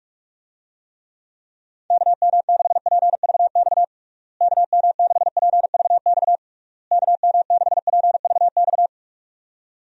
Morse code, heard as a keyed tone, about 47 wpm, 700 Hz. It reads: KM6PVX KM6PVX KM6PVX
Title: USA Call Signs - 6 Characters - Repeated 3x - 45wpm